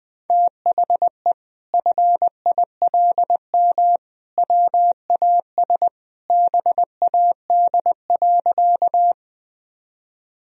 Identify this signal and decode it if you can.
Morse code, heard as a keyed tone, about 19 wpm, 700 Hz. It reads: THE FILM WAS BAD.
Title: Sentences from Top 1000 Words - Review Part 3 - 20wpm